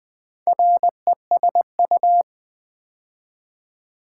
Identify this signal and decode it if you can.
Morse code, heard as a keyed tone, about 20 wpm, 700 Hz. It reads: RESU
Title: Build A Word - 2 Letters At A Time - 6-Letter Words - 20wpm